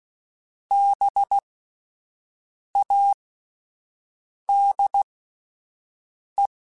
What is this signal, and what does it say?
Morse code, heard as a keyed tone, about 16 wpm, spaced as for 5 wpm, 800 Hz. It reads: BADE
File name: Spaced Alphas / Spaced Alphas mp3s / Bs sp @10